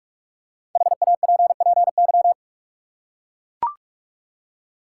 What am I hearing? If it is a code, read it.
Morse code, heard as a keyed tone, about 43 wpm, 700 Hz. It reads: HAPPY
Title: Five Letter Words - 45wpm